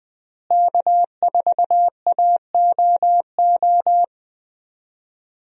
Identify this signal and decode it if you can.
Morse code, heard as a keyed tone, about 20 wpm, 700 Hz. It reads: K4AOO
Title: USA Call Signs - 5 Characters - Repeated 3x - 20wpm